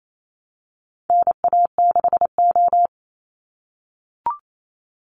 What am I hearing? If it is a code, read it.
Morse code, heard as a keyed tone, about 28 wpm, 700 Hz. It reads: NA6O
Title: NA SOTA Chaser Callsigns 28wpm